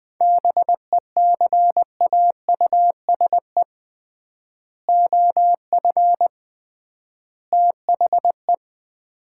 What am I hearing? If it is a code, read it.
Morse code, heard as a keyed tone, about 21 wpm, 700 Hz. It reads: BECAUSE OF THE
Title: 3x Word-Spacing - Sets of 3 Words - 20wpm